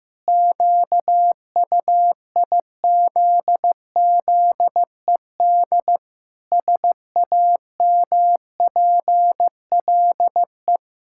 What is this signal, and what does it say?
Morse code, heard as a keyed tone, about 15 wpm, 700 Hz. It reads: QUIZZED SAMPLE
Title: Pangram Challenge v2 - 15wpm